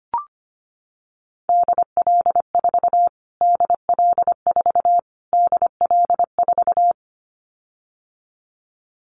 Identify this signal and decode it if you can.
Morse code, heard as a keyed tone, about 25 wpm, 700 Hz. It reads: DL4 DL4 DL4
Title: Callsign Prefixes for Top 25 DXCC - Repeated 3x - 25wpm